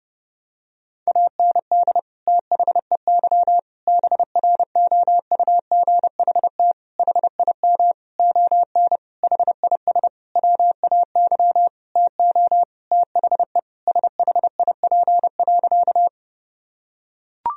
Morse code, heard as a keyed tone, about 32 wpm, 700 Hz. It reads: AND THEY BROUGHT HIM ON HIS WAY TO THE SHIP.
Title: Sentences from Top 1000 Words - Review Part 2 - 30wpm